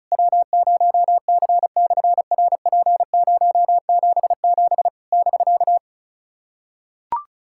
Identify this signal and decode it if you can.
Morse code, heard as a keyed tone, about 36 wpm, 700 Hz. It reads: W0C/RP077 <BK>
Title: SOTA QSO Practice 35wpm